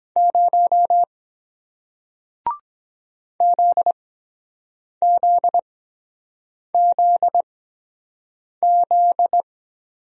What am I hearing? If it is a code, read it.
Morse code, heard as a keyed tone, about 24 wpm, 700 Hz. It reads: 0 Z Z Z Z